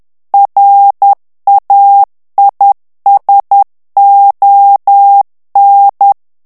Morse code, heard as a keyed tone, about 10 wpm, 800 Hz. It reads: RAISON
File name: morse mp3 / FRANCAIS TEXTE CLAIR 2 VITESSE 10 MOTS MINUTES